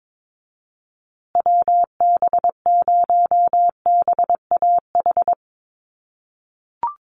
Morse code, heard as a keyed tone, about 22 wpm, 700 Hz. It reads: WB0BAH
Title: USA Call Signs Repeated 3x 22wpm